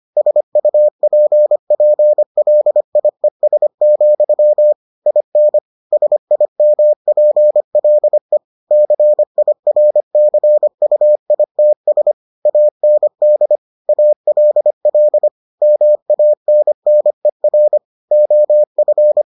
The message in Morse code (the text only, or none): SUPPLIES, IN SIMPLE CIRCUITS AND ALL MANNER OF